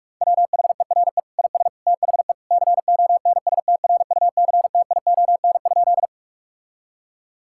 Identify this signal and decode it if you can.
Morse code, heard as a keyed tone, about 42 wpm, 700 Hz. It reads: WHERE IS THE CONSTRUCTION?
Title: Sentences from Top 1600 Words 45wpm